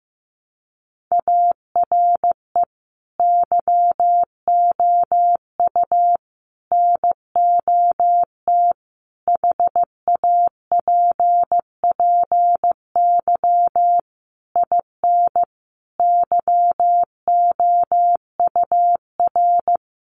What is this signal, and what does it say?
Morse code, heard as a keyed tone, about 15 wpm, 700 Hz. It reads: ARE YOU NOT HAPPY IN YOUR
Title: Sentences from Top 1000 Words 15wpm